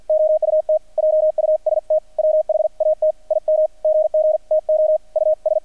Morse code, heard as a keyed tone, about 50 wpm, 620 Hz. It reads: OUT JUST WHAT IM GG TO US
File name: archived / CRSnet